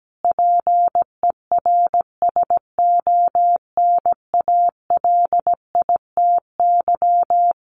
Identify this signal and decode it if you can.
Morse code, heard as a keyed tone, about 17 wpm, 700 Hz. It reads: PERSONALITY
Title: Sentences from Top 1400 Words 17wpm